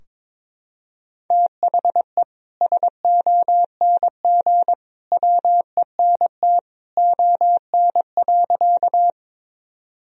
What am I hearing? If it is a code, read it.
Morse code, heard as a keyed tone, about 23 wpm, 700 Hz. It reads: THE SONG WENT ON.
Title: Sentences from Top 1000 Words - Review Part 2 - 22wpm